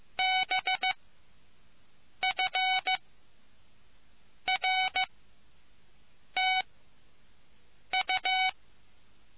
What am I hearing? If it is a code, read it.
Morse code, heard as a keyed tone, about 15 wpm, spaced as for 5 wpm, 750 Hz. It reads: BFRTU